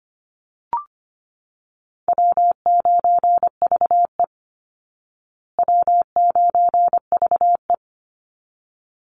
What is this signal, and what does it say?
Morse code, heard as a keyed tone, about 25 wpm, 700 Hz. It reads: W9VE W9VE
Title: Easy Call Signs Repeated 2x - 25wpm